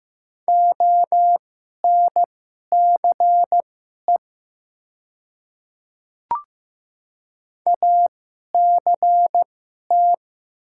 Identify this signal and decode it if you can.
Morse code, heard as a keyed tone, about 15 wpm, spaced as for 11 wpm, 700 Hz. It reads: ONCE ACT
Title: 2x Character Spacing - Top 500 Words - 15wpm